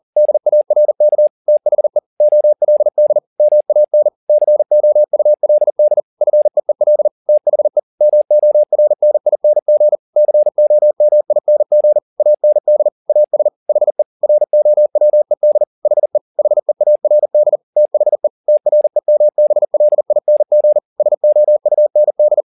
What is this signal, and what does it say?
Morse code, heard as a keyed tone, about 40 wpm, 600 Hz. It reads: DARK THE OLD MAN COULD FEEL THE MORNING COMING AND AS HE ROWED HE HEARD THE TREMBLING SOUND